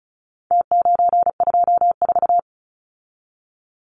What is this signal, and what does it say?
Morse code, heard as a keyed tone, about 34 wpm, 700 Hz. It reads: T924
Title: Numbers - Contest Style - 4-Digit 35wpm